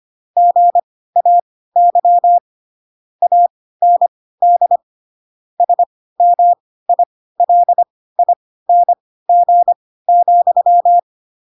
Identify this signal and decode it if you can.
Morse code, heard as a keyed tone, about 25 wpm, spaced as for 16 wpm, 700 Hz. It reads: GAY AND SMILING,